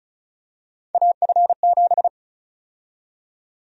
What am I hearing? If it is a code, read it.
Morse code, heard as a keyed tone, about 34 wpm, 700 Hz. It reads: AF7
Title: USA Call Sign Prefixes - 35wpm